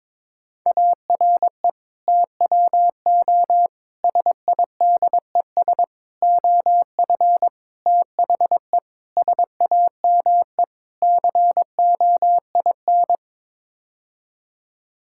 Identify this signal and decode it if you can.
Morse code, heard as a keyed tone, about 22 wpm, 700 Hz. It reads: ARE TWO SIDES OF THE SAME COIN
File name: practice / 22wpm / Sets of 7 Words - 22wpm